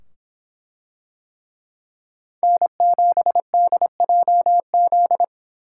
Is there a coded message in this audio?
Yes: Morse code from 2.4 to 5.2 s, heard as a keyed tone, about 25 wpm, 700 Hz. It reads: N7DJZ